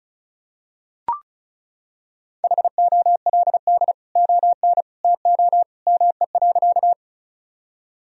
Morse code, heard as a keyed tone, about 36 wpm, 700 Hz. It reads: HOLD ON TO ME.
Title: Sentences from Top 1000 Words - Review Part 2 - 35wpm